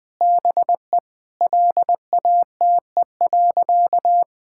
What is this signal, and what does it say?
Morse code, heard as a keyed tone, about 20 wpm, 700 Hz. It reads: BE LATE.
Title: Sentences with 6-Word Phrases - 20wpm